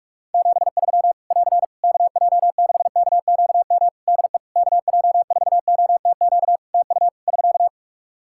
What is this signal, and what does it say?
Morse code, heard as a keyed tone, about 46 wpm, 700 Hz. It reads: 73 <AR> KJ6KQM DE KJ4OTQ TU <SK>
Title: Practice QSO - Exchange 4 - Ending - 45wpm